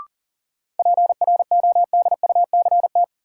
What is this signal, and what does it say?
Morse code, heard as a keyed tone, about 40 wpm, 700 Hz. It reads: PRODUCT
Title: Sentences from Top 1000 Words - Review Part 2 - New Word Drill - 40wpm